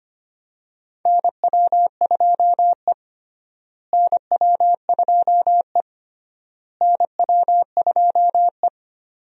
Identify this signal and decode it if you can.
Morse code, heard as a keyed tone, about 25 wpm, 700 Hz. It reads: NW2E NW2E NW2E
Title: USA Call Signs - 4 Characters - Repeated 3x - 25wpm